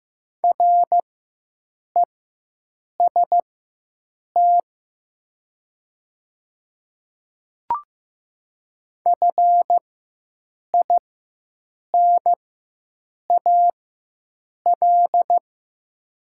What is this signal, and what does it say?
Morse code, heard as a keyed tone, about 15 wpm, spaced as for 7 wpm, 700 Hz. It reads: REST FINAL
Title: 4x Character Spacing - Top 500 Words - 15wpm